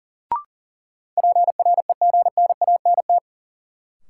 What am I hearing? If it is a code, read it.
Morse code, heard as a keyed tone, about 40 wpm, 700 Hz. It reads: PREGNANT
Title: Sentences from Top 1000 Words - Review Part 6 - New Word Drill - 40wpm